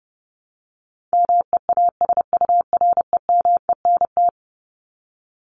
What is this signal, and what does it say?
Morse code, heard as a keyed tone, about 30 wpm, 700 Hz. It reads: MEASUREMENT
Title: Top 1751-2000 Words 30wpm